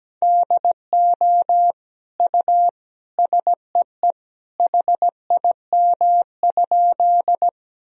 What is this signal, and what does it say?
Morse code, heard as a keyed tone, about 17 wpm, 700 Hz. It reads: DO U SEE HIM?
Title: Sentences from Top 100 Words and Abbreviations 17wpm